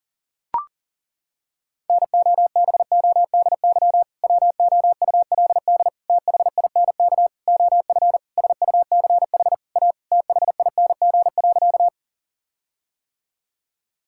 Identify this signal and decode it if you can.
Morse code, heard as a keyed tone, about 40 wpm, 700 Hz. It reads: NOBODY WOULD THINK OF SUCH A THING.
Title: Sentences from Top 1600 Words 40wpm